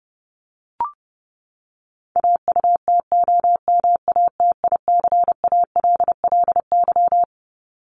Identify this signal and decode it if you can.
Morse code, heard as a keyed tone, about 30 wpm, 700 Hz. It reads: AUTOMATICALLY
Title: Top 1751-2000 Words 30wpm